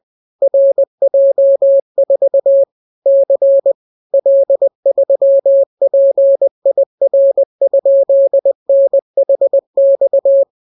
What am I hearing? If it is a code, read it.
Morse code, heard as a keyed tone, about 20 wpm, 550 Hz. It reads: RJ4 C L3PIR?NHX